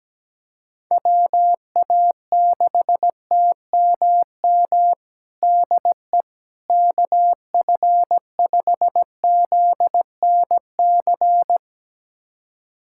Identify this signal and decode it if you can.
Morse code, heard as a keyed tone, about 17 wpm, 700 Hz. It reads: WA6TMM DE KF5ZNC
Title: USA Callsign DE Callsign - 6 Characters - 17wpm